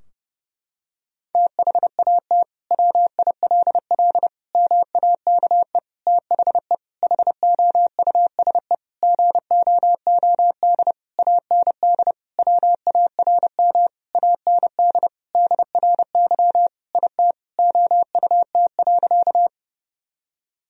Morse code, heard as a keyed tone, about 30 wpm, 700 Hz. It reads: THAT WILL MAKE THE HOUSE GOOD AND WARM AND DRY IT OUT.
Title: Sentences from Top 1000 Words - Review Part 3 - 30wpm